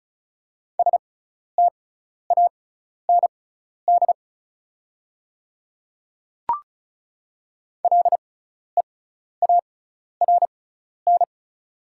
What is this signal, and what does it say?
Morse code, heard as a keyed tone, about 34 wpm, spaced as for 12 wpm, 690 Hz. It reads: STAND LEARN
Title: 6x Character Spacing - Top 300 Words - 35wpm